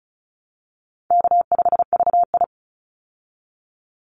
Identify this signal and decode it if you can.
Morse code, heard as a keyed tone, about 34 wpm, 700 Hz. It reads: K5VI
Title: Call Signs Repeated 2x - 35wpm